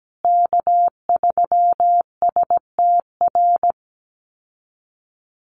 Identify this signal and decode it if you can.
Morse code, heard as a keyed tone, about 17 wpm, 700 Hz. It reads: K3STR
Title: USA Call Signs Repeated 2x 17wpm